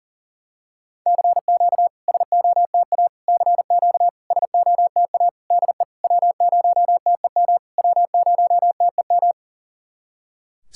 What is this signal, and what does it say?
Morse code, heard as a keyed tone, about 40 wpm, 700 Hz. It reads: CQ SOTA CQ SOTA DE W0TEM W0TEM
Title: SOTA QSO Practice 40wpm